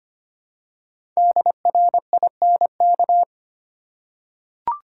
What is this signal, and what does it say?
Morse code, heard as a keyed tone, about 25 wpm, 700 Hz. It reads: DRINK
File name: 5 Letter Words - First 3 Letters Given - 25wpm